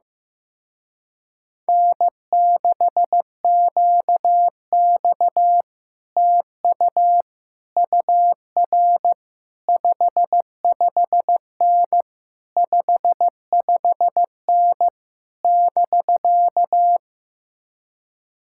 Morse code, heard as a keyed tone, about 15 wpm, 700 Hz. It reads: N6QX TU UR 55N 55N <BK>
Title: Speed-Racing Common QSO Phrases Encore - 15wpm